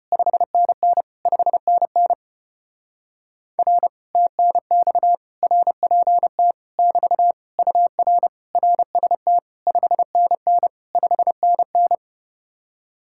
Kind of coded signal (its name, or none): Morse code